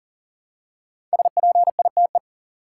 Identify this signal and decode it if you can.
Morse code, heard as a keyed tone, about 38 wpm, 700 Hz. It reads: SPITE
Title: Five Letter Words - 40wpm